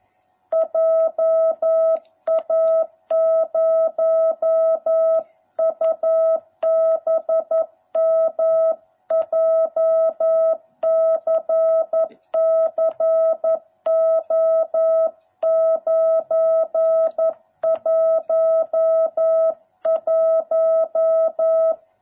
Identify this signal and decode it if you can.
Morse code, heard as a keyed tone, about 11 wpm, 650 Hz. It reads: JA0UBM JCC O911